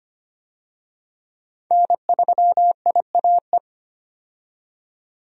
Morse code, heard as a keyed tone, about 25 wpm, 700 Hz. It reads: N3IAE
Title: USA Call Signs - 5 Characters - 25wpm